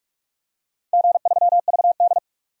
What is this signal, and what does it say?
Morse code, heard as a keyed tone, about 46 wpm, 700 Hz. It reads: G3VD
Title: DXCC-100 Call Signs - 4 Characters - 45wpm